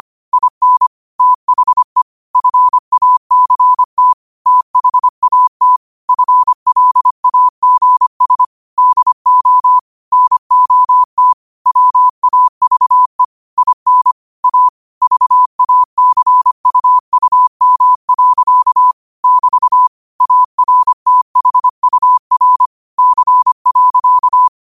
Morse code, heard as a keyed tone, about 25 wpm, 1000 Hz. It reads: IN THE FACT THAT FLAGS DO NOT WAVE IN A VACUUM. <BT> ARTHUR C.